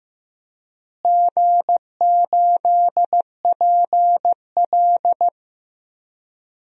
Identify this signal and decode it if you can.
Morse code, heard as a keyed tone, about 15 wpm, 700 Hz. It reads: G8PL
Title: DXCC-100 Call Signs - 4 Characters - Repeated 3x - 15wpm